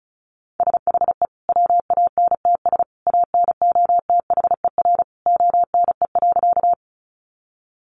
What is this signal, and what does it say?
Morse code, heard as a keyed tone, about 34 wpm, 700 Hz. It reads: SHE WANTS ANOTHER ONE.
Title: Easy Phrases 35wpm